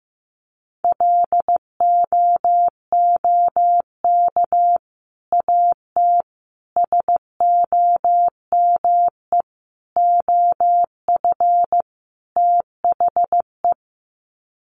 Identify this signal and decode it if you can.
Morse code, heard as a keyed tone, about 15 wpm, 700 Hz. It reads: LOOK AT SOME OF THE
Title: Sets of 5 Words Encore 15wpm